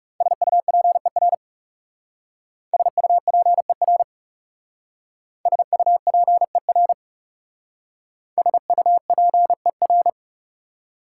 Morse code, heard as a keyed tone, about 38 wpm, 700 Hz. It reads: SUPER SUPER SUPER SUPER
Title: Speed-Racing Prefixes and Suffixes 30wpm